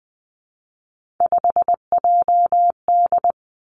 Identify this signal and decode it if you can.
Morse code, heard as a keyed tone, about 20 wpm, 700 Hz. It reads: 5JD
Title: Three Challenging Letter-Number Combinations 20wpm